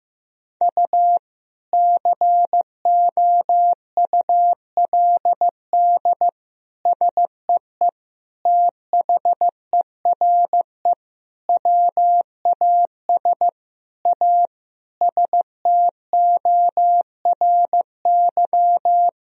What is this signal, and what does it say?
Morse code, heard as a keyed tone, about 16 wpm, 700 Hz. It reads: U COULD SEE THERE WAS A STORY
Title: Sentences from Top 500 Words and Abbreviations 15wpm